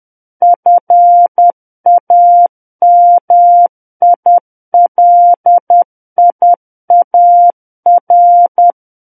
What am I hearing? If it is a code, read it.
Morse code, heard as a keyed tone, about 10 wpm, 700 Hz. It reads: FAMILIAR
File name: Book One / 01-Chapter 1 - 10 WPM